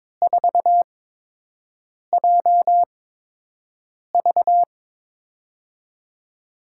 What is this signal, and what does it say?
Morse code, heard as a keyed tone, about 22 wpm, spaced as for 6 wpm, 700 Hz. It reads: 4JV
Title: USA Call Signs - 4 Characters - 8x Character Spacing - 22wpm